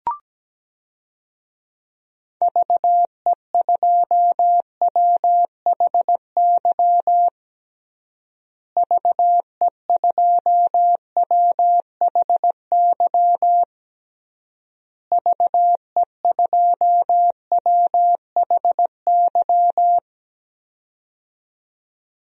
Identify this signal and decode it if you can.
Morse code, heard as a keyed tone, about 17 wpm, 700 Hz. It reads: VE2WHY VE2WHY VE2WHY
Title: DXCC-100 Call Signs - 6 Characters - Repeated 3x - 17wpm